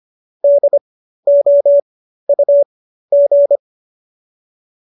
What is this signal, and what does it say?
Morse code, heard as a keyed tone, about 25 wpm, spaced as for 13 wpm, 590 Hz. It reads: DOUG